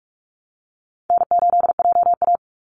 Morse code, heard as a keyed tone, about 44 wpm, 700 Hz. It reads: N8JA
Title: Call Signs - 45wpm